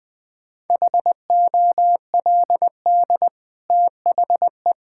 Morse code, heard as a keyed tone, about 20 wpm, 700 Hz. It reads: HOLD THE
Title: Easy Sentences with Hints - 20wpm